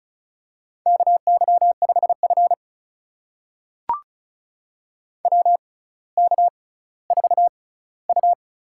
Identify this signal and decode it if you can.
Morse code, heard as a keyed tone, about 34 wpm, spaced as for 18 wpm, 700 Hz. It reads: KY5F WK4U
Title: USA Call Signs - 4 Characters - 6x Character Spacing - 35wpm